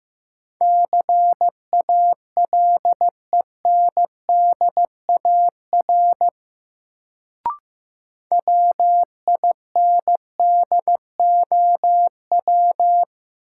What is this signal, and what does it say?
Morse code, heard as a keyed tone, about 15 wpm, 700 Hz. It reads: CALENDAR WINDOW
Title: Sentences from Top 1600 Words - New Word Drill - 15wpm